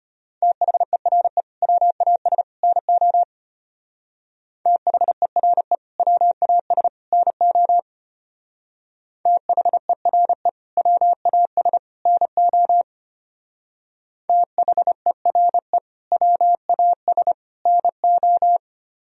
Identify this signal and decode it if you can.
Morse code, heard as a keyed tone, about 31 wpm, 700 Hz. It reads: THERE WAS NO THERE WAS NO THERE WAS NO THERE WAS NO
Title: Speed-Racing - Sets of 3 Words 25wpm